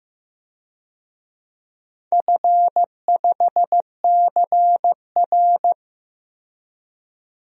Morse code, heard as a keyed tone, about 15 wpm, 700 Hz. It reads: F5CR
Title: DXCC-100 Call Signs - 4 Characters - 15wpm